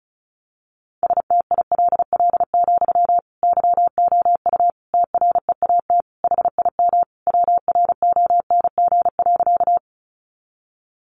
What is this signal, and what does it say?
Morse code, heard as a keyed tone, about 34 wpm, 700 Hz. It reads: STILL, YOU TREAT HIM WRONG.
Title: Sentences from Top 1300 Words 35wpm